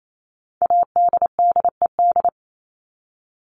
Morse code, heard as a keyed tone, about 28 wpm, 700 Hz. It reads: ADDED
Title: ED Suffix - Top 100 Words with Base Words 28wpm